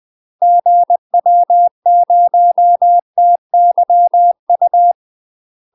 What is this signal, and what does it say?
Morse code, heard as a keyed tone, about 20 wpm, 700 Hz. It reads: GW0TYU
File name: UK MASTER QSOs TRAINING / 20WPM / UK MASTER QSOs 0041